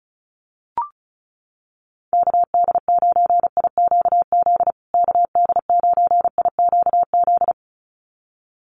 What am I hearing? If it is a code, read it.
Morse code, heard as a keyed tone, about 34 wpm, 700 Hz. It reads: KD9IQZ KD9IQZ
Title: USA Call Signs Repeated 2x 35wpm